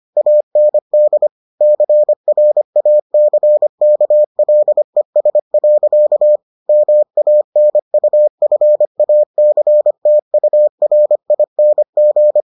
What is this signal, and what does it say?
Morse code, heard as a keyed tone, about 25 wpm, 600 Hz. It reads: AND CRACKLES. MANUFACTURING